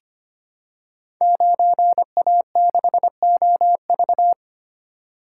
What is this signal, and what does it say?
Morse code, heard as a keyed tone, about 25 wpm, 700 Hz. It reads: 9A6OV
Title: Code Groups - 5 Letter:Numbers - 25wpm